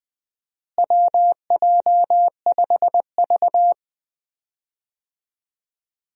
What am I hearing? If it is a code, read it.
Morse code, heard as a keyed tone, about 20 wpm, 700 Hz. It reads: WJ5V
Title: USA Call Signs - 4 Characters - Repeated 3x - 20wpm